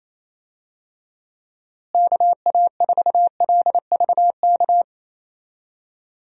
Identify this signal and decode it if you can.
Morse code, heard as a keyed tone, about 28 wpm, 700 Hz. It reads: KA4LVK
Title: USA Call Signs - 6 Characters - Repeated 3x - 28wpm